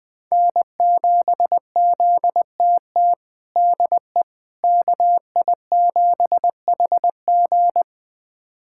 Morse code, heard as a keyed tone, about 20 wpm, 700 Hz. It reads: N7ZTT DE KI7HG
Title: USA Callsign DE Callsign - 5 Characters - 20wpm